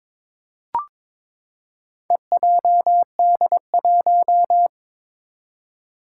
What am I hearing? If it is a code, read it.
Morse code, heard as a keyed tone, about 22 wpm, 700 Hz. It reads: EJD1
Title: Four Challenging Letter-Number Combinations 22wpm